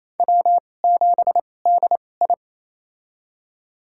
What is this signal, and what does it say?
Morse code, heard as a keyed tone, about 28 wpm, spaced as for 20 wpm, 700 Hz. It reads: W7DI
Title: USA Call Signs - 2x Character Spacing - 28wpm